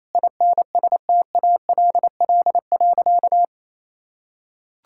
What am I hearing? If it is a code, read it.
Morse code, heard as a keyed tone, about 28 wpm, 700 Hz. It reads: INSTALL.
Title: Sentences from Top 1600 Words 28wpm